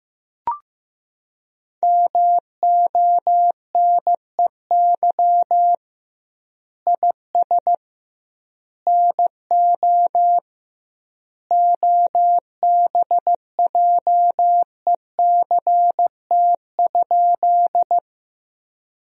Morse code, heard as a keyed tone, about 15 wpm, 700 Hz. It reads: MONEY IS NO OBJECT?
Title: 2x Word-Spacing - Sentences from Top 500 Words - 15wpm